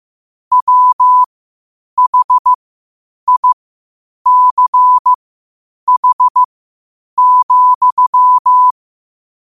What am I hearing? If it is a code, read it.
Morse code, heard as a keyed tone, about 15 wpm, spaced as for 8 wpm, 1000 Hz. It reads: WHICH,
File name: qotdfiles / 8wpm